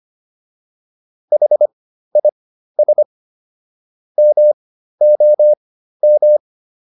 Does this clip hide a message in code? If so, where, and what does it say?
Yes: Morse code from 1.3 to 6.4 s, heard as a keyed tone, about 25 wpm, spaced as for 13 wpm, 600 Hz. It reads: HIS MOM